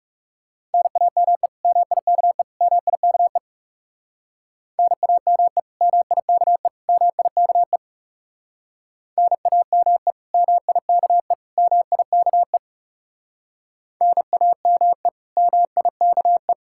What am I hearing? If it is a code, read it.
Morse code, heard as a keyed tone, about 38 wpm, 700 Hz. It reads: NAME MIKE MIKE NAME MIKE MIKE NAME MIKE MIKE NAME MIKE